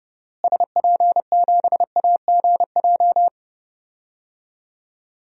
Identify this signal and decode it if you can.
Morse code, heard as a keyed tone, about 30 wpm, 700 Hz. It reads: SP7AGJ
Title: DXCC-100 Call Signs - 6 Characters - Repeated 2x - 30wpm